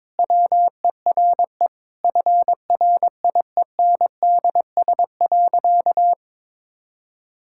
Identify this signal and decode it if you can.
Morse code, heard as a keyed tone, about 22 wpm, 700 Hz. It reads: WERE FRIENDS.
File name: practice / 22wpm / Sentences with 3-Word Phrases - 22wpm